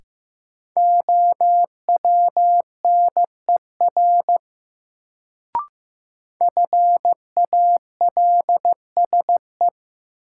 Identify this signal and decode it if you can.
Morse code, heard as a keyed tone, about 15 wpm, 700 Hz. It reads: OWNER FALSE
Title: Sentences from Top 1600 Words - New Word Drill Alt - 15wpm